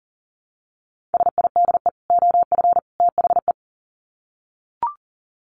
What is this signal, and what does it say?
Morse code, heard as a keyed tone, about 40 wpm, 700 Hz. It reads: SIDE OF THE
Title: Sets of 3 Words 40wpm